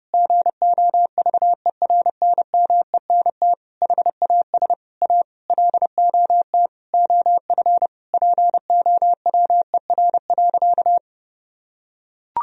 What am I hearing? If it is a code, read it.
Morse code, heard as a keyed tone, about 30 wpm, 700 Hz. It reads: GOVERNMENT HAS A LOT OF POWER.
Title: Sentences from Top 1000 Words - Review Part 4 - 30wpm